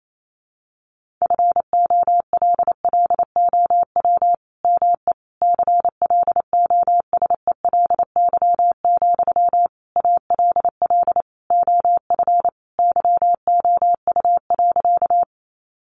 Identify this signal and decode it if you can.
Morse code, heard as a keyed tone, about 28 wpm, 700 Hz. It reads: FOLLOW ME CLOSELY, ALL OF YOU.
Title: Sentences from Top 1400 Words 28wpm